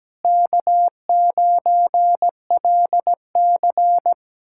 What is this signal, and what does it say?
Morse code, heard as a keyed tone, about 17 wpm, 700 Hz. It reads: K9LC
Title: K1USN SST QSO Practice - 17wpm